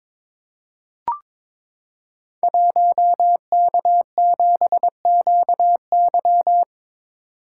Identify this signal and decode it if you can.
Morse code, heard as a keyed tone, about 23 wpm, 700 Hz. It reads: 1K7QY
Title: Code Groups - 5 Letter:Numbers - 22wpm